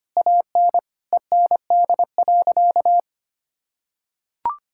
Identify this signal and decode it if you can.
Morse code, heard as a keyed tone, about 25 wpm, 700 Hz. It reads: AN END.
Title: Sentences from Top 1000 Words - Review Part 6 - 25wpm